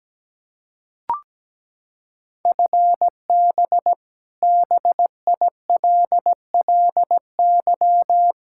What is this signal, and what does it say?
Morse code, heard as a keyed tone, about 17 wpm, 700 Hz. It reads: FB BILLY
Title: K1USN SST QSO Practice 17wpm